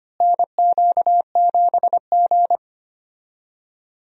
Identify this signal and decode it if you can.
Morse code, heard as a keyed tone, about 25 wpm, 700 Hz. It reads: NQ7G
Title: USA Call Signs - 4 Characters - Repeated 3x - 25wpm